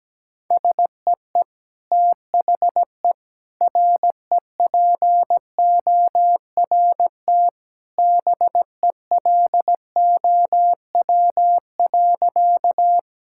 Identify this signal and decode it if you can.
Morse code, heard as a keyed tone, about 17 wpm, 700 Hz. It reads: SEE THE REPORT BELOW.
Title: Sentences from Top 1000 Words - Review Part 6 - 17wpm